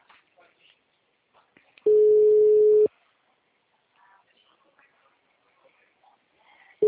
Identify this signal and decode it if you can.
Ringback tone (Europe, 425 Hz), with speech.